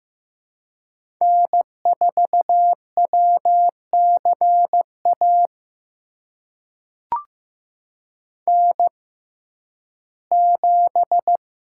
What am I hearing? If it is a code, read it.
Morse code, heard as a keyed tone, about 15 wpm, spaced as for 11 wpm, 700 Hz. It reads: N4WCA N7
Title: USA Call Signs - 5 Characters - 6x Character Spacing - 15wpm